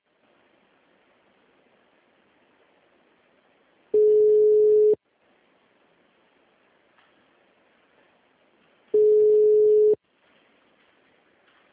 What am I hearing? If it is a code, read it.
Ringback tone (Europe, 425 Hz).